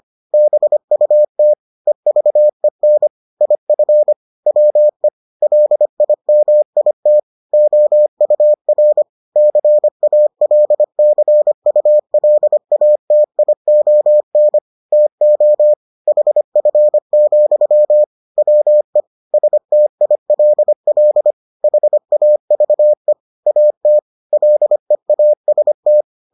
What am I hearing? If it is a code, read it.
Morse code, heard as a keyed tone, about 25 wpm, 600 Hz. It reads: BUT EVEN IF WE LIMIT OUR CALCULATION TO HF, WE STILL HAVE AT LEAST